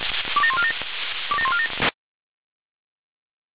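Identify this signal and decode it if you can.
Five-tone selective call (ZVEI), heard as quick runs of five tones, tones of about 70 ms. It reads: 27147 27147
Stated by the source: ZVEI five-tone call received by SDR.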